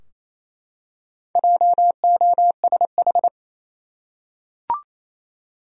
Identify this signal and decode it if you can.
Morse code, heard as a keyed tone, about 27 wpm, 700 Hz. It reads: JOSH
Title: USA Common Names - Max 4 Letters - 28wpm